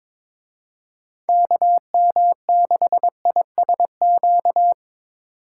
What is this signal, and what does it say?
Morse code, heard as a keyed tone, about 24 wpm, 700 Hz. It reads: KM6ISQ
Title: Speed Racing - USA 2x3 Call Signs 22wpm